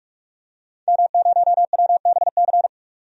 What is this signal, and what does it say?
Morse code, heard as a keyed tone, about 46 wpm, 700 Hz. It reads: M0WBC
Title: DXCC-100 Call Signs - 5 Characters - 45wpm